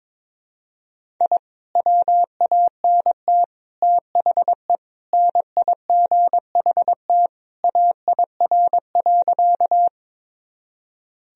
Morse code, heard as a keyed tone, about 22 wpm, 700 Hz. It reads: I WANT THE NIGHT AIR.
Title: Sentences from Top 1000 Words - Review Part 1 - 22wpm